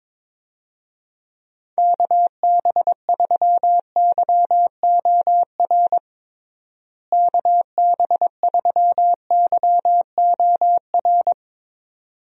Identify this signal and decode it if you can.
Morse code, heard as a keyed tone, about 21 wpm, 700 Hz. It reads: KB3YOR KB3YOR
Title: USA Call Signs - 6 Characters - Repeated 2x - 22wpm